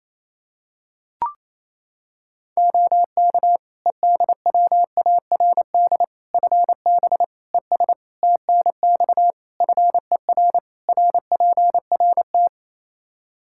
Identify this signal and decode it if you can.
Morse code, heard as a keyed tone, about 27 wpm, 700 Hz. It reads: OK EDWARD FB ES TNX FER RPRT
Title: Practice QSO - Exchange 2 - Info - 28wpm